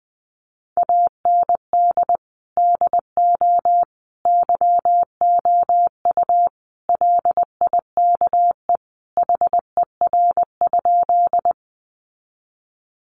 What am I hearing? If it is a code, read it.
Morse code, heard as a keyed tone, about 19 wpm, 700 Hz. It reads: AND DO YOU LIKE HER?
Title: Sentences from Top 100 Words 20wpm